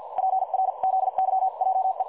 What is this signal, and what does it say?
Morse code, heard as a keyed tone, about 50 wpm, spaced as for 43 wpm, 760 Hz. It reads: PRZYJE
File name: dlazaawansowanych / nagraniazpasma / websdr recording 3565.0kHz